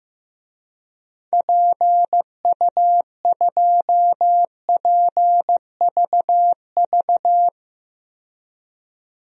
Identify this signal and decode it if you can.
Morse code, heard as a keyed tone, about 15 wpm, 700 Hz. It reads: PU2PVV
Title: DXCC-100 Call Signs - 6 Characters - 15wpm